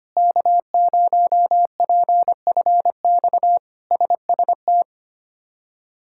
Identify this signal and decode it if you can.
Morse code, heard as a keyed tone, about 25 wpm, 700 Hz. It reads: K0PFX SST
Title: K1USN SST QSO Practice - 25wpm